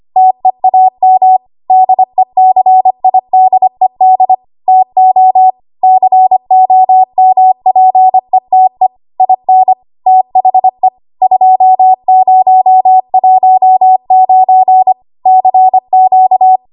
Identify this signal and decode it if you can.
Morse code, heard as a keyed tone, about 25 wpm, 750 Hz. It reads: TEAM DECIDED TO COMPETE IN THE 2019 CQ
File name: Archive / 25 WPM / 25WPM